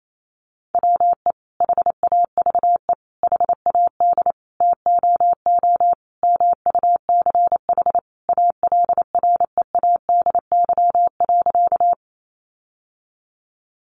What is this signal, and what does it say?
Morse code, heard as a keyed tone, about 28 wpm, 700 Hz. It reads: WE HAVE HAD TOO MUCH ALREADY.
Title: Sentences from Top 700 Words 28wpm